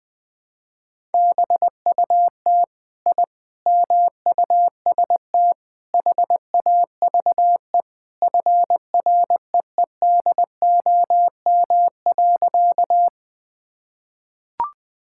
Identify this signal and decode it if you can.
Morse code, heard as a keyed tone, about 20 wpm, 700 Hz. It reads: BUT I MUST HAVE FREEDOM.
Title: Sentences from Top 1000 Words - Review Part 5 - 20wpm